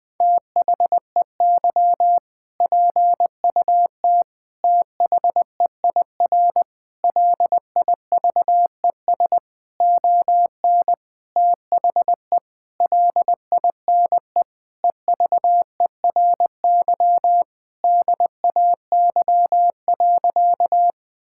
Morse code, heard as a keyed tone, about 21 wpm, 700 Hz. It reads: THEY PUT THEIR LIVES ON THE LINE EVERY DAY.
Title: Sentences with 8-Word Phrases - Spoken Beginning - 20wpm